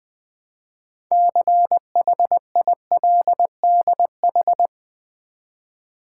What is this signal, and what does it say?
Morse code, heard as a keyed tone, about 20 wpm, 700 Hz. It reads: CHILDH
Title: Build A Word - 3 Letters At A Time - 9-Letter Words - 20wpm